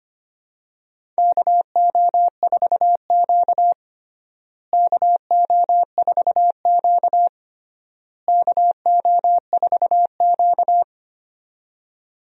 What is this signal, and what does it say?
Morse code, heard as a keyed tone, about 25 wpm, 700 Hz. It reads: KO4Q KO4Q KO4Q
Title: USA Call Signs - 4 Characters - Repeated 3x - 25wpm